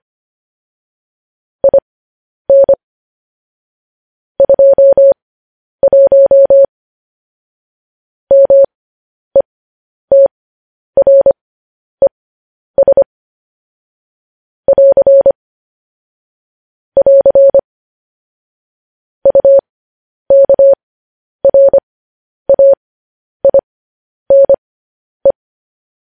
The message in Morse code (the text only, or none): IN 21 METRES <AR> <AR> UKRAINE